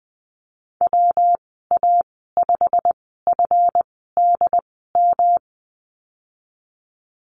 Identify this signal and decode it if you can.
Morse code, heard as a keyed tone, about 20 wpm, spaced as for 15 wpm, 700 Hz. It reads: WA5FDM
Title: 147 Lesson 24g - Letter y - Call Signs 20wpm